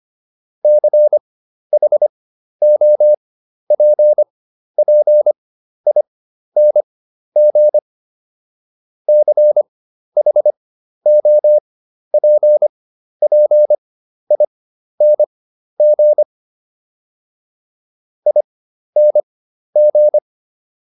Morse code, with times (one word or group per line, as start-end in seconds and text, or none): 0.6-7.8 CHOPPING
9.1-16.2 CHOPPING
18.3-20.2 ING